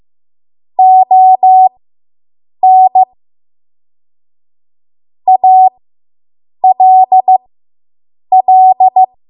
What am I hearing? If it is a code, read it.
Morse code, heard as a keyed tone, about 15 wpm, spaced as for 7 wpm, 750 Hz. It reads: ON ALL